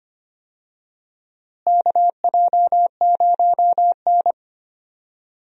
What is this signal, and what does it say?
Morse code, heard as a keyed tone, about 25 wpm, 700 Hz. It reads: KJ0N
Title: USA Call Signs - 4 Characters - Repeated 3x - 25wpm